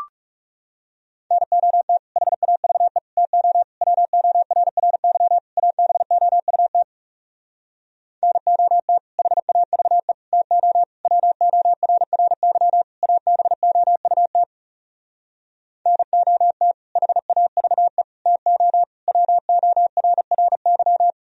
Morse code, heard as a keyed tone, about 40 wpm, 700 Hz. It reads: NOT HAVE TO WORRY ABOUT NOT HAVE TO WORRY ABOUT NOT HAVE TO WORRY